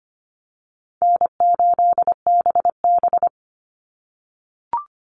Morse code, heard as a keyed tone, about 25 wpm, 700 Hz. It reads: N8BB
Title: NA SOTA Chaser Callsigns 25wpm